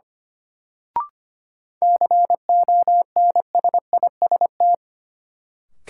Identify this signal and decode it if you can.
Morse code, heard as a keyed tone, about 25 wpm, 700 Hz. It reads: CONSIST
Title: Sentences from Top 1400 Words - New Word Drill Alt - 25wpm